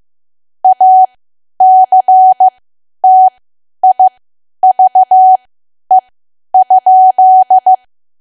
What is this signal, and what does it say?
Morse code, heard as a keyed tone, about 15 wpm, spaced as for 10 wpm, 750 Hz. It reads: ACTIVE?
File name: cwa / W1AW-Level2 / 10WPM